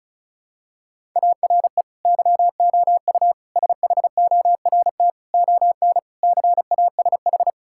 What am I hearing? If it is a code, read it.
Morse code, heard as a keyed tone, about 34 wpm, 700 Hz. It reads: ARE YOU SHORT ON CASH
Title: Sentences from Top 1000 Words - Review Part 5 - 35wpm